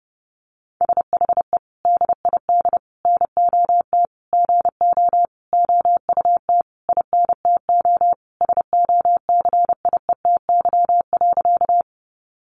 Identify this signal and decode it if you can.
Morse code, heard as a keyed tone, about 30 wpm, 700 Hz. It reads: SHE DID NOT GO OUT INTO SOCIETY.
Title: Sentences from Top 700 Words 30wpm